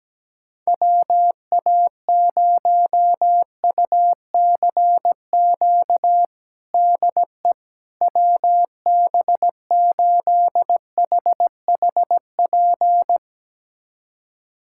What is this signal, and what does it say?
Morse code, heard as a keyed tone, about 17 wpm, 700 Hz. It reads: WA0UCQ DE WB8HHP
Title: USA Callsign DE Callsign - 6 Characters - 17wpm